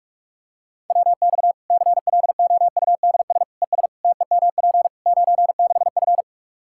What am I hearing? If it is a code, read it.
Morse code, heard as a keyed tone, about 46 wpm, 700 Hz. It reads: WX CLOUDS ES TEMP 96F
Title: Practice QSO - Exchange 2 - Info - 45wpm